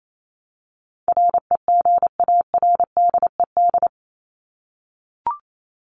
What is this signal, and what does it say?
Morse code, heard as a keyed tone, about 28 wpm, 700 Hz. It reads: REGARDED
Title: ED Suffix - Top 100 Words with Base Words 28wpm